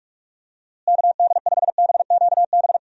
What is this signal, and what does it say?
Morse code, heard as a keyed tone, about 46 wpm, 700 Hz. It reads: KD5BQB
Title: USA Call Signs - 6 Characters - Repeated 2x - 45wpm